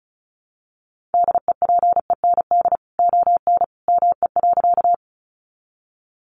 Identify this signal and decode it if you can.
Morse code, heard as a keyed tone, about 34 wpm, 700 Hz. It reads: DEPEND ON ME.
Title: Sentences from Top 1300 Words 35wpm